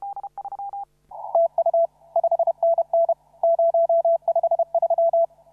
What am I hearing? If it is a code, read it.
Morse code, heard as a keyed tone, about 31 wpm, 670 Hz. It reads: TU 5NN 053
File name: CW / RHOB CW 2017 HA6P 0001